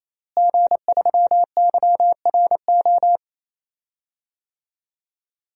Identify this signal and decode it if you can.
Morse code, heard as a keyed tone, about 28 wpm, 700 Hz. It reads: G3YRO
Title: DXCC-100 Call Signs - 5 Characters - Repeated 3x - 28wpm